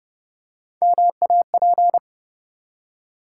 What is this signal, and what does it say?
Morse code, heard as a keyed tone, about 29 wpm, 700 Hz. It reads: MAP
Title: Three Letter Words - 30wpm